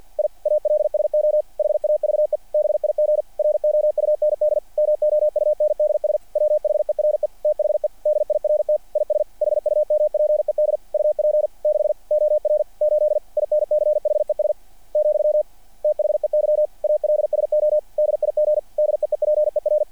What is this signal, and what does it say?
Morse code, heard as a keyed tone, about 48 wpm, 600 Hz. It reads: I ALSO HAVE BIG ROUND MOUNDS WHERE THE DIRT IS HUMPED UP 6 OR 8 INCHES , THEY ALSO DIG DEEPER
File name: archived / CRSnet